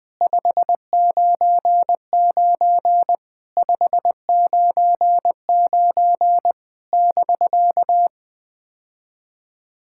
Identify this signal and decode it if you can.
Morse code, heard as a keyed tone, about 20 wpm, 700 Hz. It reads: 599 599 <BK>
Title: POTA QSO Practice - 20wpm